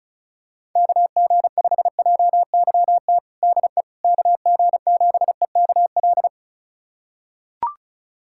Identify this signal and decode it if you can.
Morse code, heard as a keyed tone, about 36 wpm, 690 Hz. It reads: KG5JYT DE KG7EKL
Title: USA Callsign DE Callsign 35wpm